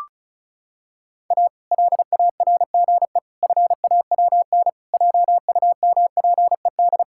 Morse code, heard as a keyed tone, about 34 wpm, 700 Hz. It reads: A LARGE FAWN JUMPED
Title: Pangram Challenge v2 - 35wpm